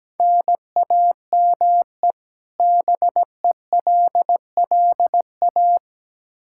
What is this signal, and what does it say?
Morse code, heard as a keyed tone, about 17 wpm, 700 Hz. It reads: NAME BELLA
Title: Practice QSO - Exchange 1 - Beginning - 17wpm